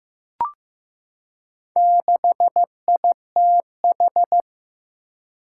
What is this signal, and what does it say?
Morse code, heard as a keyed tone, about 15 wpm, 700 Hz. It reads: 6ITH
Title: Four Challenging Letter-Number Combinations 15wpm